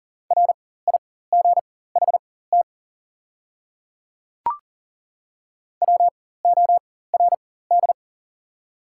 Morse code, heard as a keyed tone, about 40 wpm, spaced as for 19 wpm, 690 Hz. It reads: RIGHT WORD
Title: 4x Character Spacing - Top 200 Words - 40wpm